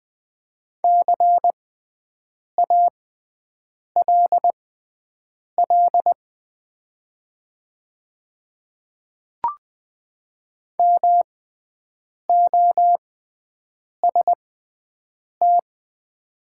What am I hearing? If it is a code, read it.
Morse code, heard as a keyed tone, about 20 wpm, spaced as for 7 wpm, 690 Hz. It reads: CALL MOST
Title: 6x Character Spacing - Top 100 Words - 20wpm